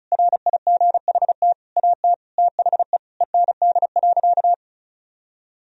Morse code, heard as a keyed tone, about 36 wpm, 700 Hz. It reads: RIGHT AT THE END.
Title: Sentences from Top 1000 Words - Review Part 2 - 35wpm